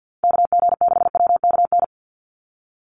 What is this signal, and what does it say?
Morse code, heard as a keyed tone, about 50 wpm, 700 Hz. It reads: KG6WKN
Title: USA Call Signs 50wpm